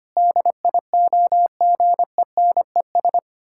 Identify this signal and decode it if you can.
Morse code, heard as a keyed tone, about 25 wpm, 700 Hz. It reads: DIOGENES
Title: Pangram Challenge v2 - 25wpm